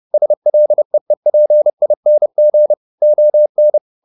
Morse code, heard as a keyed tone, about 30 wpm, 600 Hz. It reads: SLEEPING ON